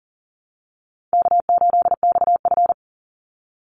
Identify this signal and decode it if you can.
Morse code, heard as a keyed tone, about 40 wpm, 700 Hz. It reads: K8XF
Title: Easy Call Signs Repeated 2x - 40wpm